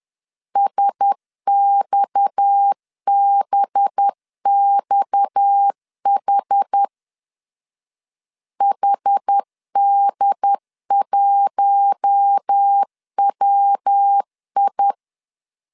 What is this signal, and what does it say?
Morse code, heard as a keyed tone, about 10 wpm, 800 Hz. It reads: SXBXH HD1WI
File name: LEZIONI / 6 PKX0 / 6 PKX 10wpm 2426